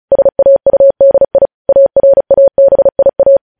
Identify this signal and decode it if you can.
Morse code, heard as a keyed tone, about 36 wpm, 550 Hz. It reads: SAUDI ARABIA
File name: podcast / News headlines-1x-35x35